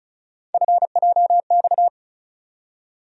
Morse code, heard as a keyed tone, about 36 wpm, 700 Hz. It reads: FJX
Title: USA Call Sign Postfixes - 3 Letters 35wpm